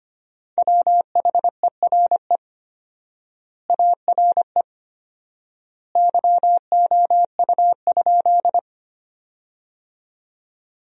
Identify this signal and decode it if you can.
Morse code, heard as a keyed tone, about 25 wpm, 700 Hz. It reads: WHERE ARE YOU?
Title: 4x Word-Spacing - Sentences from Top 200 Words - 25wpm